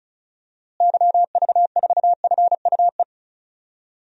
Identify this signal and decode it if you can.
Morse code, heard as a keyed tone, about 34 wpm, 700 Hz. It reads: YV4FUE
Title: DXCC-100 Call Signs - 6 Characters - Repeated 2x - 35wpm